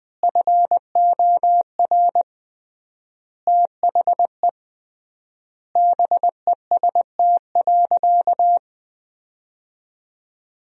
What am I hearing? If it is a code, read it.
Morse code, heard as a keyed tone, about 21 wpm, 700 Hz. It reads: FOR THE BEST.
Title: 3x Word-Spacing - Sentences from Top 500 Words - 20wpm